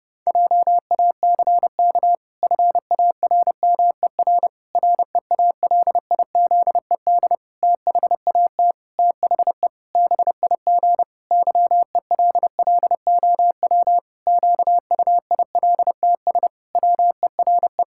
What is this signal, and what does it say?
Morse code, heard as a keyed tone, about 30 wpm, 700 Hz. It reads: JACK FARMER REALIZED THAT THE BIG YELLOW QUILTS WERE
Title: Pangram Challenge v2 - 30wpm